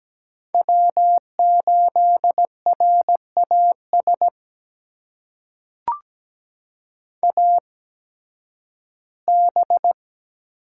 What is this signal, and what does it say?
Morse code, heard as a keyed tone, about 17 wpm, spaced as for 11 wpm, 700 Hz. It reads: W8RAS AB
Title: USA Call Signs - 5 Characters - 8x Character Spacing - 17wpm